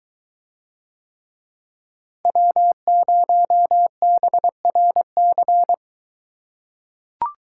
Morse code, heard as a keyed tone, about 23 wpm, 700 Hz. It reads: W0BRC